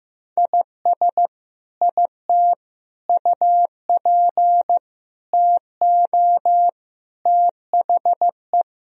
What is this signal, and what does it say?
Morse code, heard as a keyed tone, about 15 wpm, 700 Hz. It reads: IS IT UP TO THE
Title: Sentences from Top 1000 Words - Review Part 4 - 15wpm